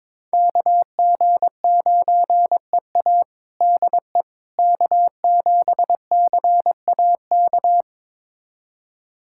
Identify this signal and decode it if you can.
Morse code, heard as a keyed tone, about 22 wpm, 700 Hz. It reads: KG9EA DE K7CAK
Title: USA Callsign DE Callsign - 5 Characters - 22wpm